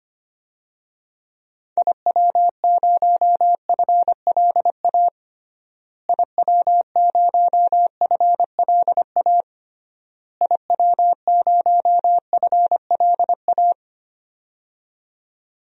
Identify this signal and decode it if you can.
Morse code, heard as a keyed tone, about 25 wpm, 700 Hz. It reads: IW0FLA IW0FLA IW0FLA
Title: DXCC-100 Call Signs - 6 Characters - Repeated 3x - 25wpm